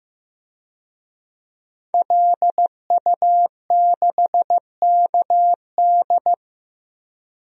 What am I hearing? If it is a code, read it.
Morse code, heard as a keyed tone, about 15 wpm, 700 Hz. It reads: LU6KD
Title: DXCC-100 Call Signs - 5 Characters - Repeated 2x - 15wpm